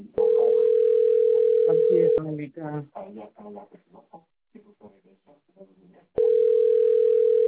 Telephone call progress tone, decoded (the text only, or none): ringback tone